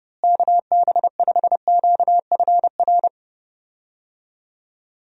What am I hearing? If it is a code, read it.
Morse code, heard as a keyed tone, about 30 wpm, 700 Hz. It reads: KB5QFR
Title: USA Call Signs - 6 Characters - Repeated 3x - 30wpm